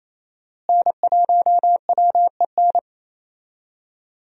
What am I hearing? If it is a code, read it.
Morse code, heard as a keyed tone, about 27 wpm, 700 Hz. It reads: N1WEN
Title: USA Call Signs - 5 Characters - Repeated 2x - 28wpm